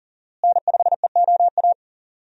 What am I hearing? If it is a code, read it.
Morse code, heard as a keyed tone, about 40 wpm, 700 Hz. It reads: N5EOA
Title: USA Call Signs - 5 Characters - Repeated 2x - 40wpm